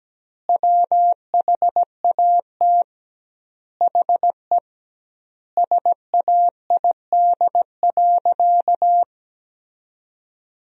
Morse code, heard as a keyed tone, about 17 wpm, 700 Hz. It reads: WHAT HE SAID.
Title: 2x Word-Spacing - Sentences from Top 400 Words - 17wpm